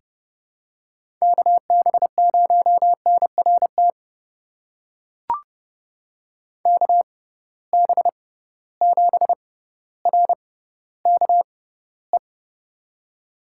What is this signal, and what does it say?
Morse code, heard as a keyed tone, about 31 wpm, spaced as for 15 wpm, 700 Hz. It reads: KB0NRT KB7RKE
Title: USA Call Signs - 6 Characters - 6x Character Spacing - 30wpm